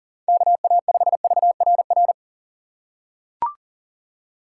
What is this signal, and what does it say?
Morse code, heard as a keyed tone, about 37 wpm, 700 Hz. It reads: KA5VRR
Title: USA Call Signs - 6 Characters - 4x Character Spacing - 40wpm